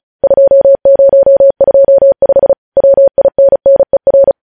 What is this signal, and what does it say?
Morse code, heard as a keyed tone, about 36 wpm, 550 Hz. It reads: 2025 WINNER
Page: News headlines in Morse code at 35 wpm